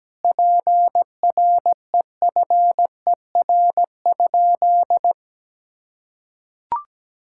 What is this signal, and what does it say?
Morse code, heard as a keyed tone, about 17 wpm, 700 Hz. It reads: PREFER?
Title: Sentences from Top 1600 Words 17wpm